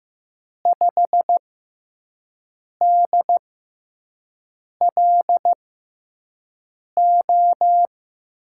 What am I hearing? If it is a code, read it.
Morse code, heard as a keyed tone, about 15 wpm, spaced as for 5 wpm, 700 Hz. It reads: 5DLO
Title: USA Call Signs - 5 Characters - 6x Character Spacing - 15wpm